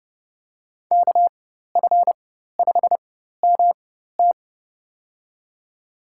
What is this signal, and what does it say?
Morse code, heard as a keyed tone, about 30 wpm, spaced as for 14 wpm, 700 Hz. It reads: KF5MT
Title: USA Call Signs - 5 Characters - 4x Character Spacing - 30wpm